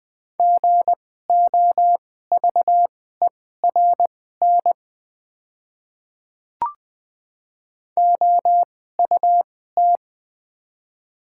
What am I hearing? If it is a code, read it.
Morse code, heard as a keyed tone, about 20 wpm, spaced as for 14 wpm, 700 Hz. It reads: GOVERN OUT
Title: 2x Character Spacing - Top 500 Words - 20wpm